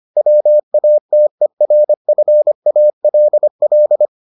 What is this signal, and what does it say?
Morse code, heard as a keyed tone, about 25 wpm, 600 Hz. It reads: WATERFALL